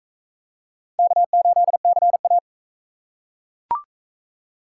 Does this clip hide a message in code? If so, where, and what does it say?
Yes: Morse code from 1.0 to 2.4 s, heard as a keyed tone, about 39 wpm, 700 Hz. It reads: K8CA